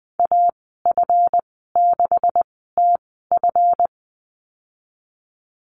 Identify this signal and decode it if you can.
Morse code, heard as a keyed tone, about 20 wpm, spaced as for 14 wpm, 700 Hz. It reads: AF6TF
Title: 154 Lesson 25g - Letter p - Call Signs 20wpm